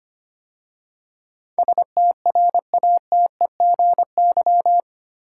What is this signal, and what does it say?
Morse code, heard as a keyed tone, about 25 wpm, 700 Hz. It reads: STRATEGY
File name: 8 Letter Words - First 3 Letters Given - 25wpm